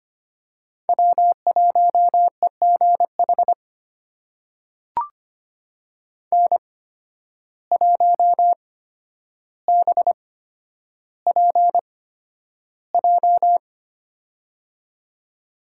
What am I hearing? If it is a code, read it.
Morse code, heard as a keyed tone, about 25 wpm, spaced as for 11 wpm, 700 Hz. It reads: W1EGH N1BPJ
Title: USA Call Signs - 5 Characters - 8x Character Spacing - 25wpm